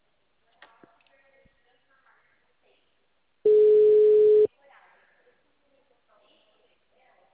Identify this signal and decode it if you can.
Ringback tone (Europe, 425 Hz), with speech.
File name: out- -2013- - - .44782